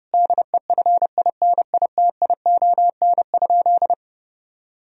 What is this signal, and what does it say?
Morse code, heard as a keyed tone, about 30 wpm, 700 Hz. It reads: DEFINITION?
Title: Sentences from Top 1000 Words - Review Part 4 - 30wpm